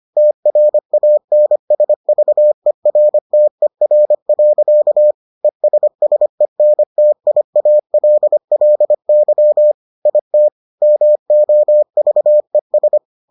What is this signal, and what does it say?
Morse code, heard as a keyed tone, about 25 wpm, 600 Hz. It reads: TRANSVERTER. ESSENTIALLY IT MOVES